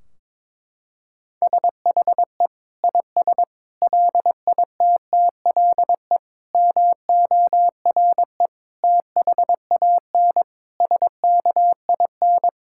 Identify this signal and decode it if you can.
Morse code, heard as a keyed tone, about 22 wpm, 700 Hz. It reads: SHE IS LITTLE MORE THAN SKIN
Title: Sentences from Top 1600 Words 22wpm